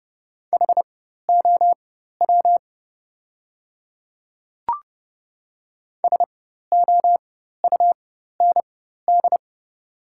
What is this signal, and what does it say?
Morse code, heard as a keyed tone, about 30 wpm, spaced as for 14 wpm, 700 Hz. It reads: HOW SOUND
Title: 4x Character Spacing - Top 100 Words - 30wpm